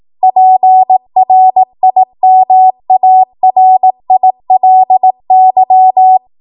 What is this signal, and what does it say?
Morse code, heard as a keyed tone, about 18 wpm, 750 Hz. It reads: PRIMARILY
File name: Morse / 18WPM